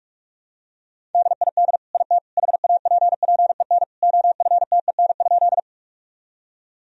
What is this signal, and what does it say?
Morse code, heard as a keyed tone, about 45 wpm, 700 Hz. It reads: DID IT HAPPEN OFTEN?
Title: Sentences from Top 1000 Words - Review Part 1 - 45wpm